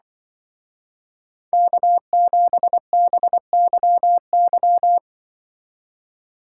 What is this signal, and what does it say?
Morse code, heard as a keyed tone, about 24 wpm, 700 Hz. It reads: K7BYY
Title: Speed Racing - USA 1x3 Call Signs 20wpm